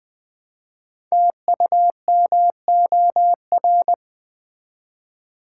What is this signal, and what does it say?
Morse code, heard as a keyed tone, about 20 wpm, 700 Hz. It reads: TUMOR
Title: Build A Word - 1 Letter At A Time - 5-Letter Words - 20wpm